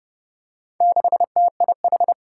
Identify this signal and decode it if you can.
Morse code, heard as a keyed tone, about 29 wpm, 700 Hz. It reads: 6TIH
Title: Four Challenging Letter-Number Combinations 30wpm